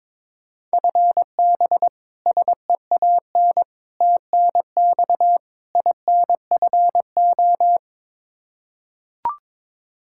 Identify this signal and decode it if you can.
Morse code, heard as a keyed tone, about 22 wpm, 700 Hz. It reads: FB SEAN TNX INFO
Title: Practice QSO - Exchange 3 - More Info - 22wpm